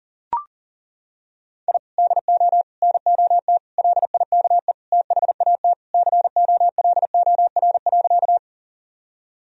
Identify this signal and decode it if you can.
Morse code, heard as a keyed tone, about 37 wpm, 700 Hz. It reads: I DO NOT LIKE THAT COLOR.
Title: Sentences from Top 1000 Words - Review Part 2 - 40wpm